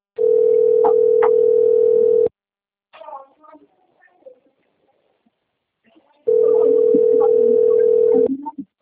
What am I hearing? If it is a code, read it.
Ringback tone (North America), 440 + 480 Hz.
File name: rg-1600-unknown- - - .17940